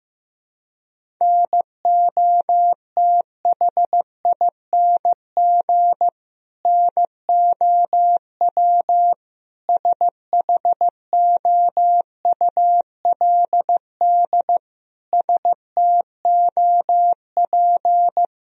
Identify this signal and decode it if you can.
Morse code, heard as a keyed tone, about 15 wpm, 700 Hz. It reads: NOTHING NOW SHOULD STOP
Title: Sentences from Top 1000 Words - Review Part 1 - 15wpm